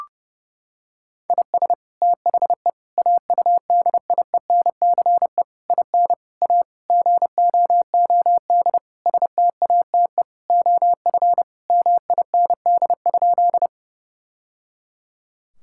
Morse code, heard as a keyed tone, about 30 wpm, 700 Hz. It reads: IS THE AUDIENCE IN A GOOD STATE OF MIND?
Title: Sentences from Top 1000 Words - Review Part 5 - 30wpm